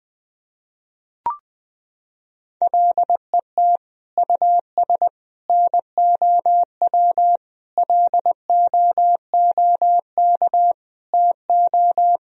Morse code, heard as a keyed tone, about 20 wpm, 700 Hz. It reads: LET US NOW LOOK TO
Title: Sentences from Top 1000 Words - Review Part 1 - 20wpm